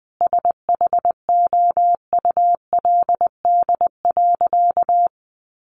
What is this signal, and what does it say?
Morse code, heard as a keyed tone, about 20 wpm, 700 Hz. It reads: SHOULD.
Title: Sentences from Top 700 Words 20wpm